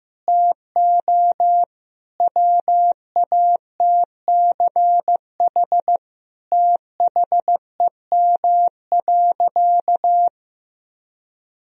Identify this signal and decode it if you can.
Morse code, heard as a keyed tone, about 15 wpm, 700 Hz. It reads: TO WATCH THEM.
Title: Sentences from Top 1000 Words - Review Part 4 - 15wpm